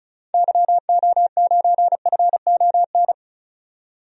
Morse code, heard as a keyed tone, about 34 wpm, 700 Hz. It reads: YO9FON
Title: DXCC-100 Call Signs - 6 Characters - 35wpm